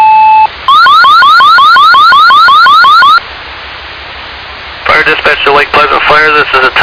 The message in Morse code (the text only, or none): none